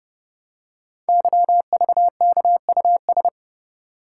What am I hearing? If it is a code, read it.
Morse code, heard as a keyed tone, about 30 wpm, 700 Hz. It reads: YVKUS
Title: Code Groups - 5 Letters - 30wpm